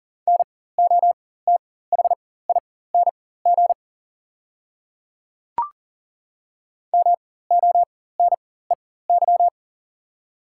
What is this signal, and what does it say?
Morse code, heard as a keyed tone, about 37 wpm, spaced as for 18 wpm, 700 Hz. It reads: NOTHING MONEY
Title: 4x Character Spacing - Top 500 Words - 40wpm